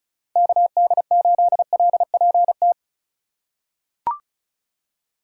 Morse code, heard as a keyed tone, about 34 wpm, 700 Hz. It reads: KD8LPT
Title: USA Call Signs - 6 Characters - 2x Character Spacing - 35wpm